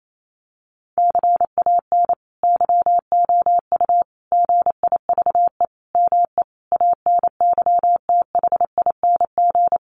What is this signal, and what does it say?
Morse code, heard as a keyed tone, about 28 wpm, 700 Hz. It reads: CAN YOU GIVE ME ANYTHING
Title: Sentences from Top 1500 Words 28wpm